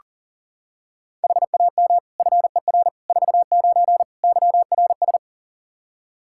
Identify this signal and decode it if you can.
Morse code, heard as a keyed tone, about 40 wpm, 700 Hz. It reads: HAM FER 49 YRS
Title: Practice QSO - Exchange 3 - More Info - 40wpm